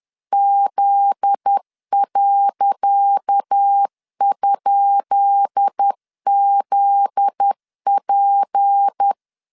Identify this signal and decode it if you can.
Morse code, heard as a keyed tone, about 10 wpm, 800 Hz. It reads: Z.?ZP
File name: LEZIONI / 10 SIGNS / 10D COMA 10wpm 3013